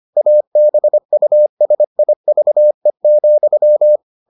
Morse code, heard as a keyed tone, about 25 wpm, 600 Hz. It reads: ABUSIVE,